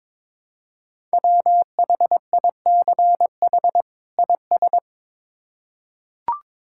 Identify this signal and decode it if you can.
Morse code, heard as a keyed tone, about 22 wpm, 700 Hz. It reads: WHICH IS
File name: Build A Phrase - 2-Word Phrases - 22wpm